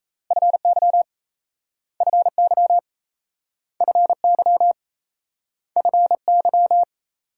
Morse code, heard as a keyed tone, about 37 wpm, 700 Hz. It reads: FY FY FY FY